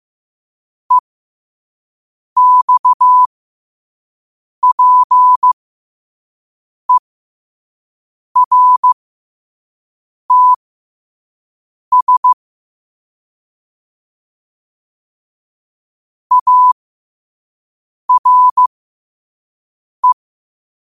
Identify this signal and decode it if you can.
Morse code, heard as a keyed tone, about 15 wpm, spaced as for 5 wpm, 1000 Hz. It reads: EXPERTS ARE
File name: qotdfiles / 5wpm